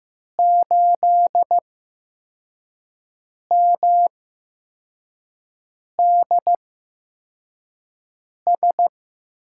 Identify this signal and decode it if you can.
Morse code, heard as a keyed tone, about 15 wpm, spaced as for 4 wpm, 700 Hz. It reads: 8MDS
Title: USA Call Signs - 8x Character Spacing - 15wpm